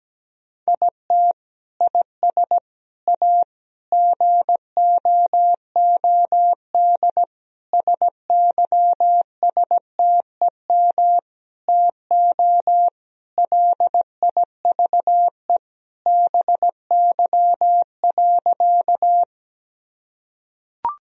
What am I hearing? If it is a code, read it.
Morse code, heard as a keyed tone, about 17 wpm, 700 Hz. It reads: IT IS A GOOD SYSTEM TO LIVE BY.
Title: Sentences from Top 1000 Words - Review Part 3 - 17wpm